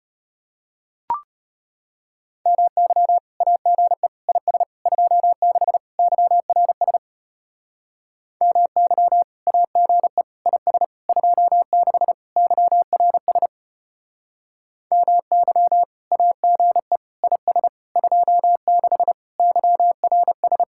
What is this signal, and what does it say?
Morse code, heard as a keyed tone, about 31 wpm, 700 Hz. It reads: MY AGE IS 26 YRS MY AGE IS 26 YRS MY AGE IS 26 YRS
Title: Speed-Racing Common QSO Phrases - 25wpm